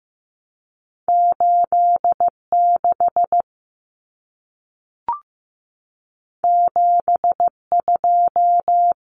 Morse code, heard as a keyed tone, about 15 wpm, 700 Hz. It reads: 86 72
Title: Common Cut Numbers - 2-Digit 15wpm